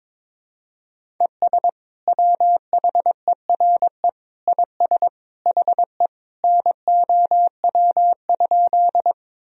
Morse code, heard as a keyed tone, about 21 wpm, 700 Hz. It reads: ES WHERE IS HE NOW?
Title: Sentences from Top 200 Words and Abbreviations 22wpm